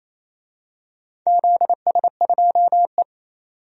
Morse code, heard as a keyed tone, about 27 wpm, 690 Hz. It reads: ZS2E
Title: DXCC-100 Call Signs - 4 Characters - 28wpm